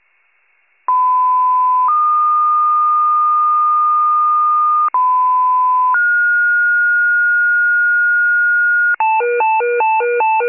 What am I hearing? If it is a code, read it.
Two-tone page, heard as a one-second tone, then a three-second tone, about 1010 then 1220 Hz, about 1010 then 1430 Hz, with noise. It is heard from 0.9 to 9.0 s.